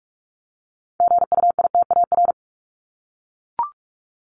Morse code, heard as a keyed tone, about 44 wpm, 700 Hz. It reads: GUITAR
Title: Top 1501-1750 Words 45wpm